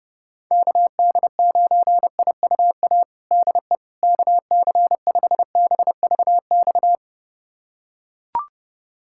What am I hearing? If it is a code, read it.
Morse code, heard as a keyed tone, about 31 wpm, 700 Hz. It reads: KD9IUA DE KC5BVX
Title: USA Callsign DE Callsign 30wpm